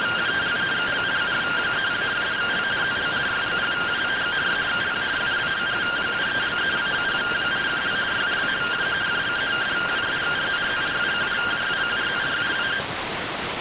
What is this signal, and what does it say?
Two-tone teleprinter signal, about 170 Hz shift.